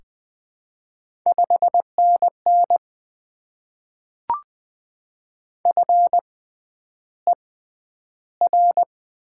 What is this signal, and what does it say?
Morse code, heard as a keyed tone, about 20 wpm, spaced as for 10 wpm, 700 Hz. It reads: 5NN FER
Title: Top 20 QSO Elements - 6x Character Spacing - 20wpm